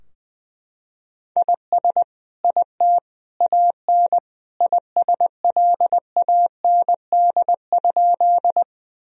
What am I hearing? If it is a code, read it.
Morse code, heard as a keyed tone, about 21 wpm, 700 Hz. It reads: IS IT AN ISLAND?
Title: Sentences from Top 1000 Words - Review Part 3 - 20wpm